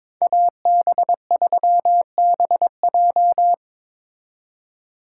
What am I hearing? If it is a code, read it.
Morse code, heard as a keyed tone, about 26 wpm, 700 Hz. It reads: AB3BJ
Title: Speed Racing - USA 2x2 Call Signs 22wpm